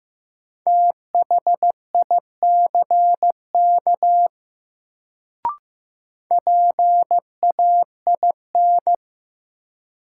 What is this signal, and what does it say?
Morse code, heard as a keyed tone, about 15 wpm, 700 Hz. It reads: THICK PAIN
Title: Sentences from Top 1600 Words - New Word Drill Alt - 15wpm